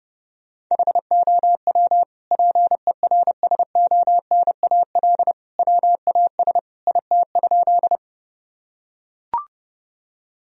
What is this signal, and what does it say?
Morse code, heard as a keyed tone, about 30 wpm, 700 Hz. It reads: HOW PERSONAL WAS IT?
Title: Sentences from Top 1000 Words - Review Part 4 - 30wpm